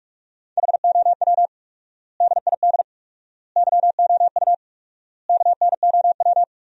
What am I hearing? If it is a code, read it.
Morse code, heard as a keyed tone, about 46 wpm, 700 Hz. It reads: HOW DID YOU KNOW
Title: 4x Word-Spacing - Sentences from Top 100 Words - 45wpm